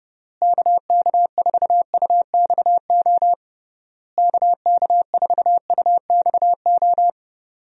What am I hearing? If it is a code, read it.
Morse code, heard as a keyed tone, about 30 wpm, 700 Hz. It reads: KK4UXO KK4UXO
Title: USA Call Signs - 6 Characters - Repeated 3x - 30wpm